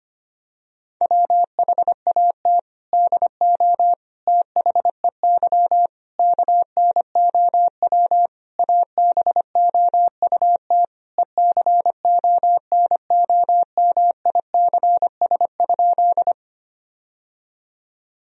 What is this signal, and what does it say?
Morse code, heard as a keyed tone, about 25 wpm, 700 Hz. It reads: WHAT DO THEY KNOW ABOUT ECONOMICS?
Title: Sentences from Top 1000 Words - Review Part 3 - 25wpm